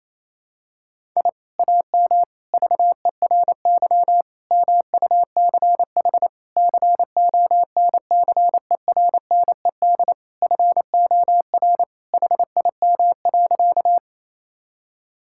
Morse code, heard as a keyed tone, about 28 wpm, 700 Hz. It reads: I AM VERY MUCH CONCERNED FOR HIM.
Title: Sentences from Top 1000 Words - Review Part 6 - 28wpm